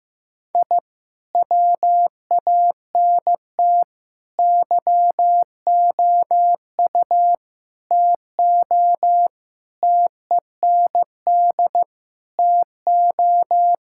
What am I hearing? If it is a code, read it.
Morse code, heard as a keyed tone, about 15 wpm, 700 Hz. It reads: I WANT YOU TO TEND TO
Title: Sentences from Top 1000 Words - Review Part 5 - 15wpm